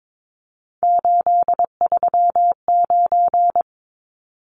Morse code, heard as a keyed tone, about 22 wpm, 700 Hz. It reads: 839
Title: Numbers - 3-Digit 22wpm